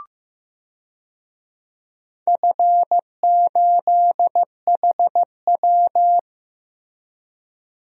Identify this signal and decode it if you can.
Morse code, heard as a keyed tone, about 15 wpm, 700 Hz. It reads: F8HW
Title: DXCC-100 Call Signs - 4 Characters - Repeated 3x - 15wpm